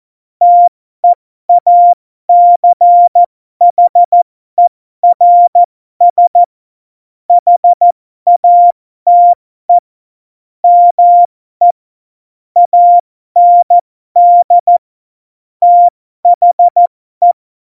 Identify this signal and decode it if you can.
Morse code, heard as a keyed tone, about 14 wpm, spaced as for 12 wpm, 700 Hz. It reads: TEACHERS HATE ME AND THE
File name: Morsekurs / Witze